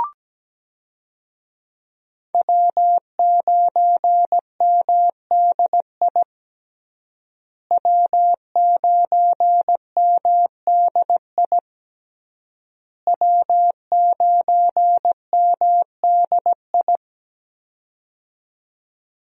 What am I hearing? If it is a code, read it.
Morse code, heard as a keyed tone, about 17 wpm, 700 Hz. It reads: W9MDI W9MDI W9MDI
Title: USA Call Signs - 5 Characters - Repeated 3x - 17wpm